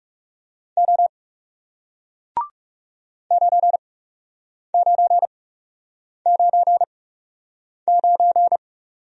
Morse code, heard as a keyed tone, about 38 wpm, spaced as for 8 wpm, 700 Hz. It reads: O 9999